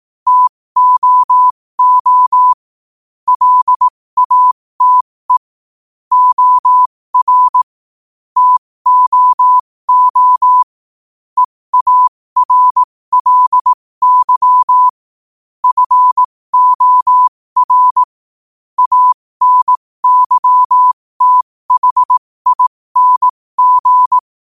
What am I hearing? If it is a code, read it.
Morse code, heard as a keyed tone, about 18 wpm, spaced as for 15 wpm, 1000 Hz. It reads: TOO LATE OR TOO EARLY FOR ANYTHING
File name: qotdfiles / 15wpm